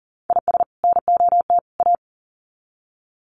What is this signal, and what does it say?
Morse code, heard as a keyed tone, about 40 wpm, 700 Hz. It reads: IS NOT A
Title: Sets of 3 Words 40wpm